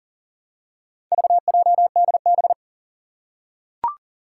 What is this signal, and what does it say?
Morse code, heard as a keyed tone, about 40 wpm, 700 Hz. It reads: VJDB
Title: Four Challenging Letter-Number Combinations 40wpm